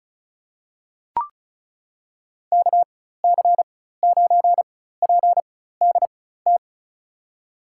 Morse code, heard as a keyed tone, about 36 wpm, spaced as for 16 wpm, 700 Hz. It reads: KC9PDT
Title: USA Call Signs - 6 Characters - 4x Character Spacing - 35wpm